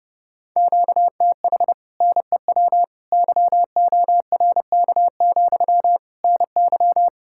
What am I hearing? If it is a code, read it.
Morse code, heard as a keyed tone, about 30 wpm, 700 Hz. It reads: QTH NEW YORK, NY